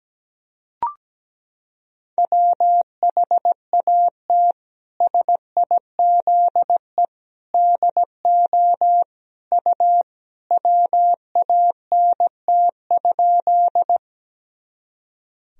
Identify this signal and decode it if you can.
Morse code, heard as a keyed tone, about 17 wpm, 700 Hz. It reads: WHAT SIZE DO U WANT?
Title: Sentences from Top 400 Words and Abbreviations 17wpm